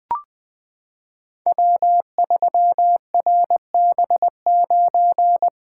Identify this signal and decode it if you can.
Morse code, heard as a keyed tone, about 20 wpm, 700 Hz. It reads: W3RB9
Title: Code Groups - 5 Letter:Numbers - 20wpm